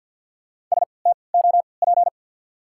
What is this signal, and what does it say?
Morse code, heard as a keyed tone, about 53 wpm, spaced as for 29 wpm, 700 Hz. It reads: STOP
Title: 3x Character Spacing - Top 500 Words - 50wpm